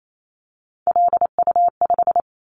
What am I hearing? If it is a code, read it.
Morse code, heard as a keyed tone, about 28 wpm, 700 Hz. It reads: LU5
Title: Callsign Prefixes for Top 50 DXCC - 28wpm